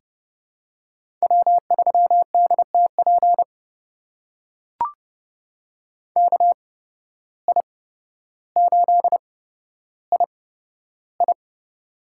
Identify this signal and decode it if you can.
Morse code, heard as a keyed tone, about 30 wpm, spaced as for 13 wpm, 700 Hz. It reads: W3DTP KI8II
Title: USA Call Signs - 5 Characters - 8x Character Spacing - 30wpm